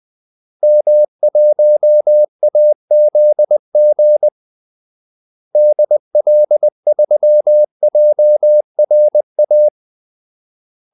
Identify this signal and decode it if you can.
Morse code, heard as a keyed tone, about 20 wpm, 600 Hz. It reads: M1AZG DL3JRA
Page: Kaldesignaler 20 wpm